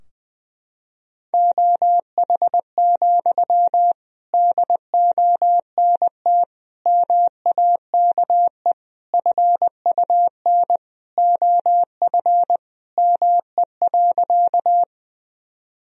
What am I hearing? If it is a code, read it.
Morse code, heard as a keyed tone, about 20 wpm, 700 Hz. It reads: OH, DONT MAKE FUN OF ME.
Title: Sentences from Top 1000 Words - Review Part 4 - 20wpm